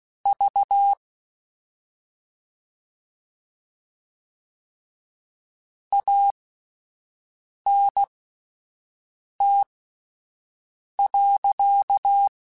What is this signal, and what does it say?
Morse code, heard as a keyed tone, about 16 wpm, spaced as for 5 wpm, 800 Hz. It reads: V ANT.
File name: QSOs / QSOs-mp3s / WD0FBT 1@10